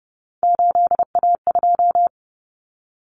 Morse code, heard as a keyed tone, about 30 wpm, 700 Hz. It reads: 8A2
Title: Common Cut Numbers - 3-Digit 30wpm